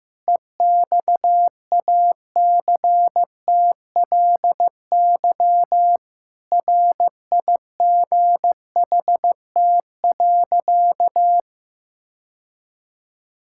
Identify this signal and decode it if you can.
Morse code, heard as a keyed tone, about 15 wpm, 700 Hz. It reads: EXACTLY RIGHT.
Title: Sentences from Top 1000 Words - Review Part 6 - 15wpm